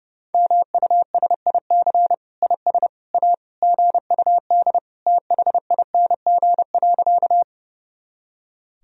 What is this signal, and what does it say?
Morse code, heard as a keyed tone, about 30 wpm, 700 Hz. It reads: MUSIC IS A GUD THING.
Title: Sentences from Top 300 Words and Abbreviations 30wpm